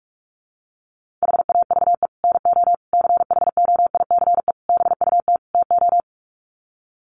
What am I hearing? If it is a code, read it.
Morse code, heard as a keyed tone, about 44 wpm, 700 Hz. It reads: HAVE NO CHOICE BUT TO
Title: Sets of 5 Words Encore 45wpm